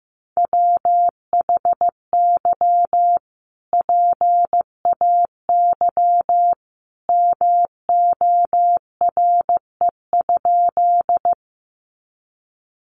Morse code, heard as a keyed tone, about 15 wpm, 700 Hz. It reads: WHY PAY MORE?
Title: Sentences from Top 700 Words 15wpm